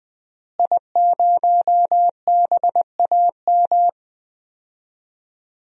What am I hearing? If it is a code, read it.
Morse code, heard as a keyed tone, about 20 wpm, 700 Hz. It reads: I0BAM
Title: DXCC-100 Call Signs - 5 Characters - Repeated 3x - 20wpm